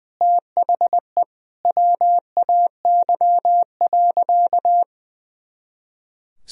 Morse code, heard as a keyed tone, about 20 wpm, 700 Hz. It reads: THE WAY.
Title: Sentences from Top 300 Words and Abbreviations 20wpm